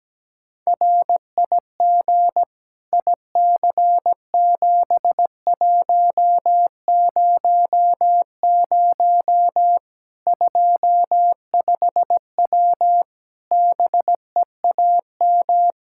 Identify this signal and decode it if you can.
Morse code, heard as a keyed tone, about 17 wpm, 700 Hz. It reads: RIG IC7100 25W BEAM
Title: Practice QSO - Exchange 2 - Info - 17wpm